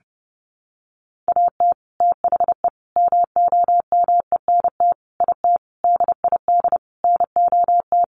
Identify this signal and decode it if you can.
Morse code, heard as a keyed tone, about 28 wpm, 700 Hz. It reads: AT THE MOMENT IT DID NOT
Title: Sentences from Top 1200 Words 30wpm